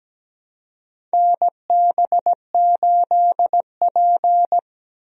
Morse code, heard as a keyed tone, about 17 wpm, 700 Hz. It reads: NB8P
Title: USA Call Signs - 4 Characters - 17wpm